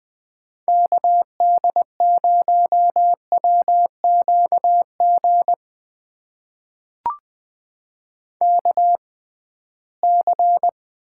Morse code, heard as a keyed tone, about 20 wpm, spaced as for 15 wpm, 700 Hz. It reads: KD0WQG KC
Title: USA Call Signs - 6 Characters - 6x Character Spacing - 20wpm